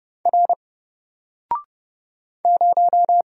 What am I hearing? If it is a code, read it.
Morse code, heard as a keyed tone, about 30 wpm, 700 Hz. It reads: R 0
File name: Single-Letter-Number-30wpm